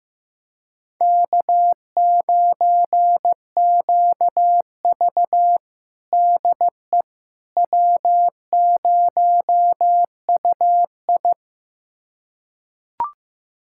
Morse code, heard as a keyed tone, about 15 wpm, 700 Hz. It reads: K9QV DE W0UI
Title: USA Callsign DE Callsign - 4 Characters - 15wpm